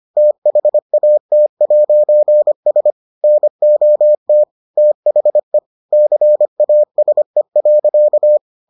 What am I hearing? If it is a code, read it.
Morse code, heard as a keyed tone, about 25 wpm, 600 Hz. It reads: THAT'S NOT THE CASE.